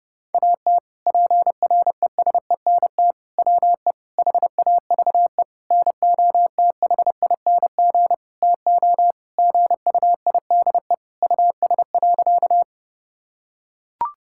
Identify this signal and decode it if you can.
Morse code, heard as a keyed tone, about 32 wpm, 700 Hz. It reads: AT PRESENT WE HAVE NOTHING TO GUIDE US.
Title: Sentences from Top 1000 Words - Review Part 5 - 30wpm